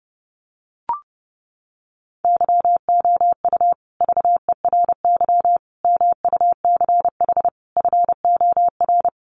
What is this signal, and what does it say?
Morse code, heard as a keyed tone, about 30 wpm, 700 Hz. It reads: YOU VERY MUCH FOR
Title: Sets of 4 Words 30wpm